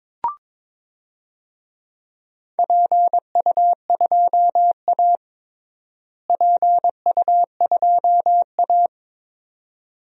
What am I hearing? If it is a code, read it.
Morse code, heard as a keyed tone, about 22 wpm, 700 Hz. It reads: PU2A PU2A
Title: DXCC-100 Call Signs - 4 Characters - Repeated 2x - 22wpm